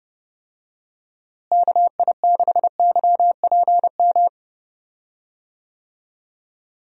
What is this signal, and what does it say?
Morse code, heard as a keyed tone, about 31 wpm, 700 Hz. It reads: KI6YPM
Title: USA Call Signs - 6 Characters - Repeated 3x - 30wpm